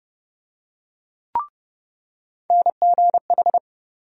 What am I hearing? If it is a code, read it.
Morse code, heard as a keyed tone, about 30 wpm, 700 Hz. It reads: NGH
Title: USA Call Sign Postfixes - 3 Letters 30wpm